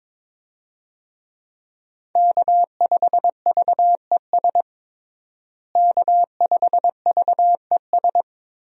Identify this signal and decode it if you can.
Morse code, heard as a keyed tone, about 22 wpm, 700 Hz. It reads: K5VES K5VES
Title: USA Call Signs - 5 Characters - Repeated 3x - 22wpm